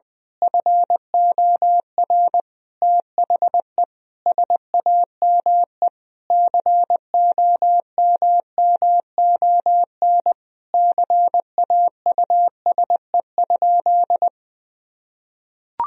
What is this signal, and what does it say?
Morse code, heard as a keyed tone, about 21 wpm, 700 Hz. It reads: FOR THE SAME COMMON CAUSE?
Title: Sentences from Top 1000 Words - Review Part 3 - 20wpm